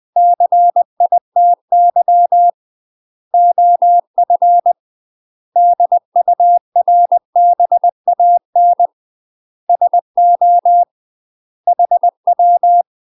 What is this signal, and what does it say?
Morse code, heard as a keyed tone, about 20 wpm, 700 Hz. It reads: CITY OF DURBAN SO HW
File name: UK MASTER QSOs TRAINING / 20WPM / UK MASTER QSOs 0036